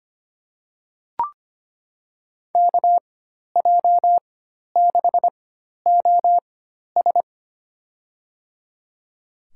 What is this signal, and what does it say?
Morse code, heard as a keyed tone, about 25 wpm, spaced as for 12 wpm, 700 Hz. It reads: KJ6OS
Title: USA Call Signs - 4x Character Spacing - 25wpm